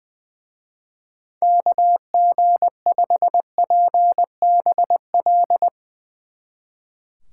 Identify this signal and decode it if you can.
Morse code, heard as a keyed tone, about 20 wpm, 700 Hz. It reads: KG5PBL
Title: Speed Racing - USA 2x3 Call Signs 20wpm